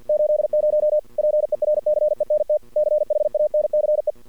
Morse code, heard as a keyed tone, about 53 wpm, 600 Hz. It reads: 80 PERCENT CHANCE
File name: archived / CRSnet